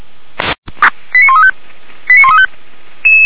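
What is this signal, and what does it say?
Five-tone selective call (ZVEI), heard as quick runs of five tones, tones of about 70 ms. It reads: 89135 89135
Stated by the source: ZVEI five-tone selective call, code 89132, received by SDR; recording hosted by feuerwehr-orlamuende.org (fire brigade Orlamünde).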